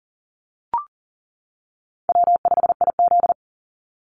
Morse code, heard as a keyed tone, about 37 wpm, 700 Hz. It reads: W5IZ
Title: Call Signs - 40wpm